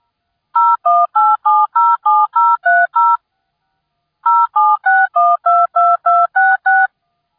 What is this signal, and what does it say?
DTMF, heard as telephone touch tones, tones of about 220 ms. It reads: 018707030 076122266